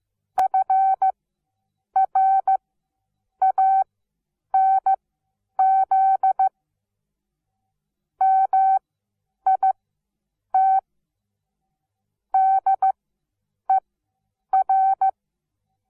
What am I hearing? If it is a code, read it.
Morse code, heard as a keyed tone, about 15 wpm, spaced as for 8 wpm, 770 Hz. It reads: FRANZ MIT DER
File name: Onlinelehrgang / morsen / DJ4UF Morsen17